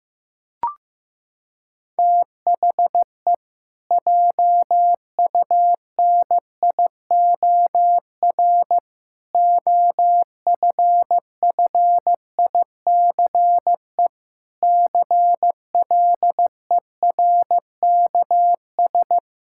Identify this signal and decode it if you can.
Morse code, heard as a keyed tone, about 15 wpm, 700 Hz. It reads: THE JUNIOR OFFICE CLERKS
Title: Pangram Challenge v2 - 15wpm